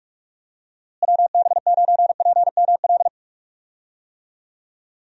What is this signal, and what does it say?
Morse code, heard as a keyed tone, about 41 wpm, 700 Hz. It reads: WB9PML
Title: USA Call Signs - 6 Characters - Repeated 2x - 45wpm